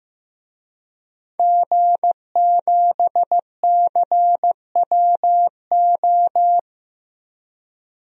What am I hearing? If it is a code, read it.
Morse code, heard as a keyed tone, about 15 wpm, 700 Hz. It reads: G7CWO
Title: DXCC-100 Call Signs - 5 Characters - Repeated 2x - 15wpm